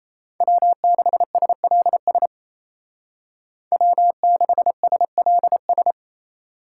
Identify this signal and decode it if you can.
Morse code, heard as a keyed tone, about 33 wpm, 700 Hz. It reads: W6SLS W6SLS
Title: Speed Racing - USA 1x3 Call Signs 28wpm